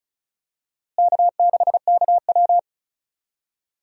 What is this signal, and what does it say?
Morse code, heard as a keyed tone, about 34 wpm, 700 Hz. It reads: K6KW
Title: USA Call Signs - 4 Characters - Repeated 2x - 35wpm